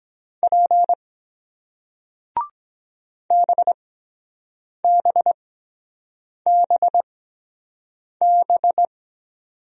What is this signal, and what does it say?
Morse code, heard as a keyed tone, about 23 wpm, spaced as for 7 wpm, 700 Hz. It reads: P BBBB